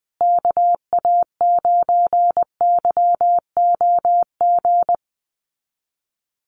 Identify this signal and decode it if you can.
Morse code, heard as a keyed tone, about 20 wpm, 700 Hz. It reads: KA9YOG
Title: USA Call Signs Repeated 2x 20wpm